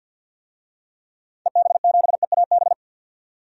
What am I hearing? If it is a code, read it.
Morse code, heard as a keyed tone, about 53 wpm, 700 Hz. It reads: EB7EAB
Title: DXCC-100 Call Signs - 6 Characters - 50wpm